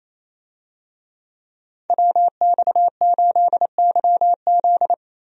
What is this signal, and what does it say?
Morse code, heard as a keyed tone, about 28 wpm, 700 Hz. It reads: WX8YZ
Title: USA Call Signs - 5 Characters - Repeated 2x - 28wpm